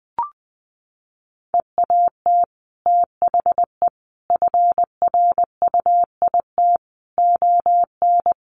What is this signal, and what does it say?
Morse code, heard as a keyed tone, about 20 wpm, 700 Hz. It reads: EAT THE FRUIT ON
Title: Sentences from Top 1300 Words 20wpm